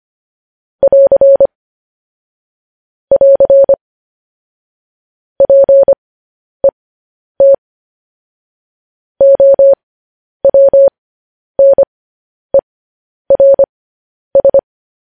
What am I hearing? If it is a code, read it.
Morse code, heard as a keyed tone, about 25 wpm, spaced as for 10 wpm, 550 Hz. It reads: <AR> <AR> PET OWNERS
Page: news-headlines-in-morse-code-at-10-wpm